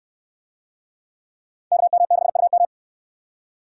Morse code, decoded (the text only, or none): KM6UM